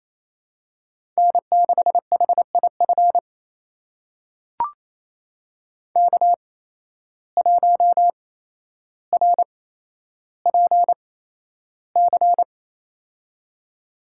Morse code, heard as a keyed tone, about 28 wpm, spaced as for 12 wpm, 700 Hz. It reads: N6HIF K1RPC
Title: USA Call Signs - 5 Characters - 8x Character Spacing - 28wpm